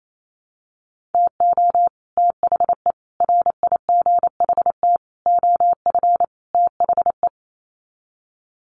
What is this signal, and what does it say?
Morse code, heard as a keyed tone, about 28 wpm, 700 Hz. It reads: TO THE RIGHT OF THE
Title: Sets of 5 Words Encore 28wpm